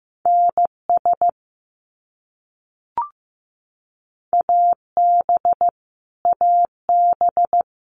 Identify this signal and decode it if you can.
Morse code, heard as a keyed tone, about 15 wpm, 700 Hz. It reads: NS AB AB
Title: Canadian Provinces and Territories - Repeated 3x - 15wpm